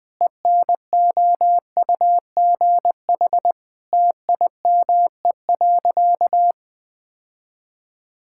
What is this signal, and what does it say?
Morse code, heard as a keyed tone, about 21 wpm, 700 Hz. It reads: ENOUGH TIME.
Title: Sentences with 4-Word Phrases - 20wpm